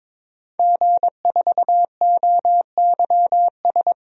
Morse code, heard as a keyed tone, about 22 wpm, 700 Hz. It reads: G4OYS
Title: DXCC-100 Call Signs - 5 Characters - Repeated 2x - 22wpm